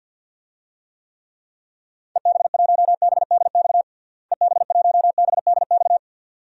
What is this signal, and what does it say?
Morse code, heard as a keyed tone, about 48 wpm, 700 Hz. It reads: EB1BDX EB1BDX
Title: DXCC-100 Call Signs - 6 Characters - Repeated 2x - 50wpm